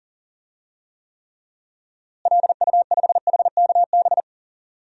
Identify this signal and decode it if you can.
Morse code, heard as a keyed tone, about 38 wpm, 700 Hz. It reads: LU5HKB
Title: DXCC-100 Call Signs - 6 Characters - Repeated 3x - 40wpm